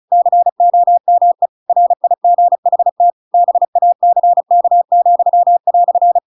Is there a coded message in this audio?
Yes: Morse code from 0.1 to 6.2 s, heard as a keyed tone, about 34 wpm, 700 Hz. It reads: COME RIGHT BACK,"